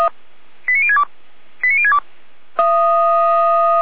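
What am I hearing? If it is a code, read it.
Five-tone selective call (ZVEI), heard as quick runs of five tones, tones of about 70 ms. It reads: 89062 89062
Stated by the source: ZVEI five-tone selective call received on an SDR.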